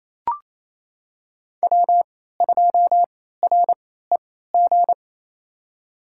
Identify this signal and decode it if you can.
Morse code, heard as a keyed tone, about 29 wpm, spaced as for 16 wpm, 700 Hz. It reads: W2REG
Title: USA Call Signs - 3x Character Spacing - 28wpm